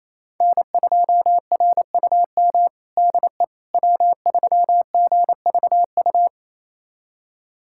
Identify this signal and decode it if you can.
Morse code, heard as a keyed tone, about 28 wpm, 700 Hz. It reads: N2RUM DE W3GVU
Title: USA Callsign DE Callsign - 5 Characters - 28wpm